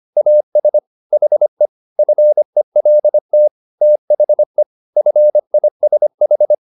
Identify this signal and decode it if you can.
Morse code, heard as a keyed tone, about 25 wpm, 600 Hz. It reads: AS HE FELT THE FISH